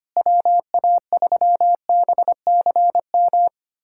Morse code, heard as a keyed tone, about 25 wpm, 700 Hz. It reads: WA3BCM
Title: USA Callsign DE Callsign 25wpm